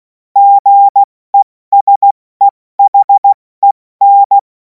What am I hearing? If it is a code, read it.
Morse code, heard as a keyed tone, about 16 wpm, spaced as for 14 wpm, 800 Hz. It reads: GESEHEN
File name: Morsekurs / Witze / Witz 10. März 2026 70 F80